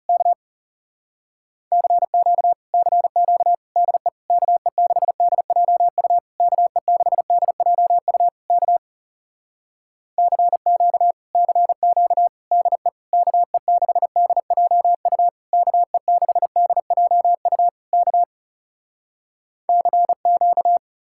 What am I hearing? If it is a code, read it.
Morse code, heard as a keyed tone, about 37 wpm, 700 Hz. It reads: K CQ CQ DE KE6DJU KE6DJU K CQ CQ DE KE6DJU KE6DJU K CQ
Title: Speed-Racing Common QSO Phrases - 30wpm